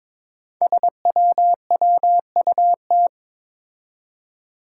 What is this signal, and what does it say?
Morse code, heard as a keyed tone, about 23 wpm, 700 Hz. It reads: SWWUT
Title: Five Challenging Letter-Number Combinations 22wpm